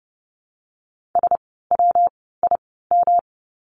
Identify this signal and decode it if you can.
Morse code, heard as a keyed tone, about 30 wpm, 700 Hz. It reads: S W I M
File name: course / 30:10wpm / 115 Lesson 19f - Letter w - Words 30wpm